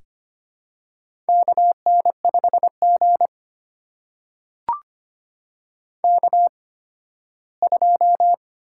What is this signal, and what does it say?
Morse code, heard as a keyed tone, about 25 wpm, spaced as for 15 wpm, 700 Hz. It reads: KN5G K2
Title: USA Call Signs - 4 Characters - 8x Character Spacing - 25wpm